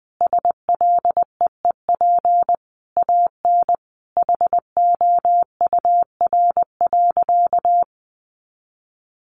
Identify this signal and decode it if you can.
Morse code, heard as a keyed tone, about 20 wpm, 700 Hz. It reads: SLEEP AN HOUR.
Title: Sentences from Top 500 Words 20wpm